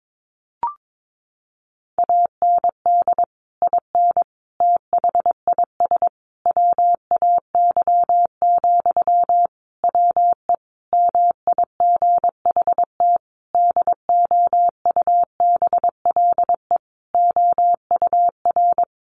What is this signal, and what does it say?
Morse code, heard as a keyed tone, about 22 wpm, 700 Hz. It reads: AND IN THIS WAY, WE MIGHT DOUBLE OUR
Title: Sentences from Top 1400 Words 22wpm